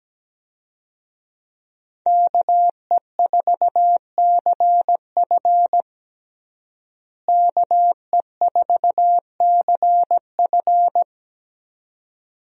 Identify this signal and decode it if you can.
Morse code, heard as a keyed tone, about 17 wpm, 700 Hz. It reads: KE4CF KE4CF
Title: USA Call Signs - 5 Characters - Repeated 2x - 17wpm